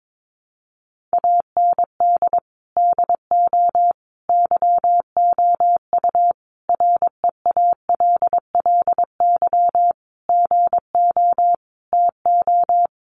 Morse code, heard as a keyed tone, about 22 wpm, 700 Hz. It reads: AND DO YOU REALLY GO TO
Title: Sentences from Top 600 Words 22wpm